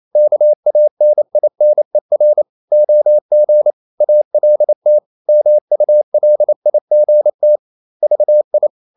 Morse code, heard as a keyed tone, about 28 wpm, 600 Hz. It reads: KANINER OG ALT MULIGT VI
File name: Ferie QRQ 28wpm